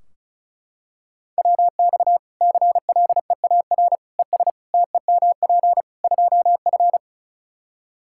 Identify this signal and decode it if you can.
Morse code, heard as a keyed tone, about 35 wpm, 700 Hz. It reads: WX CLEAR ES TEMP 2F
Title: Practice QSO - Exchange 2 - Info - 35wpm